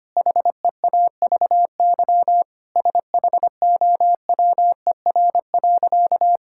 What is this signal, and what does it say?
Morse code, heard as a keyed tone, about 25 wpm, 700 Hz. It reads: HEAVY SHOWER.
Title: Sentences from Top 1600 Words 25wpm